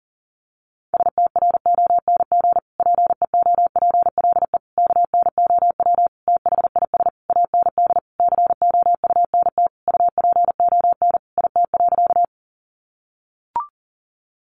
Morse code, heard as a keyed tone, about 37 wpm, 700 Hz. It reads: STRONG PEOPLE KNOW THIS AND COUNT UPON IT.
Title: Sentences from Top 1500 Words 40wpm